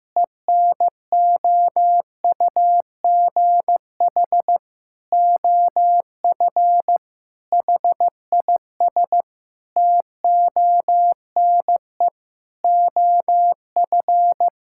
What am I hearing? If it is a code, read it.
Morse code, heard as a keyed tone, about 15 wpm, 700 Hz. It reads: ENOUGH OF HIS TONE OF
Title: Sentences from Top 1600 Words 15wpm